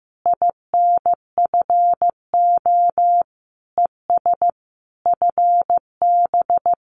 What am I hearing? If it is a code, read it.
Morse code, heard as a keyed tone, about 15 wpm, 700 Hz. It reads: INFO ES FB
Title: Common QSO Phrases Encore 15wpm